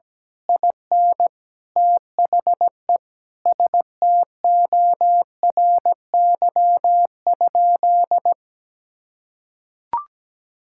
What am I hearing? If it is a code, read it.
Morse code, heard as a keyed tone, about 17 wpm, 700 Hz. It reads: IN THE STORY?
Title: Sentences from Top 1000 Words - Review Part 5 - 17wpm